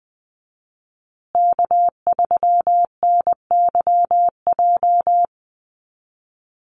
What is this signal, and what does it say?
Morse code, heard as a keyed tone, about 20 wpm, 700 Hz. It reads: K3NYJ
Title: USA Call Signs Repeated 3x 20wpm